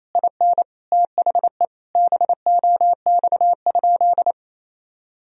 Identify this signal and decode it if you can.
Morse code, heard as a keyed tone, about 28 wpm, 700 Hz. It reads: IN THE BOX?
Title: Sentences from Top 1000 Words - Review Part 2 - 28wpm